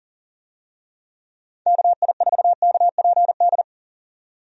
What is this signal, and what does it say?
Morse code, heard as a keyed tone, about 37 wpm, 700 Hz. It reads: KI4KPD
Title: USA Call Signs - 6 Characters - 40wpm